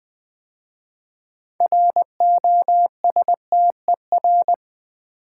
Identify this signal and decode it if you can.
Morse code, heard as a keyed tone, about 20 wpm, 700 Hz. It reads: ROSTER
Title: 6 Letter Words - First 3 Letters Given - 20wpm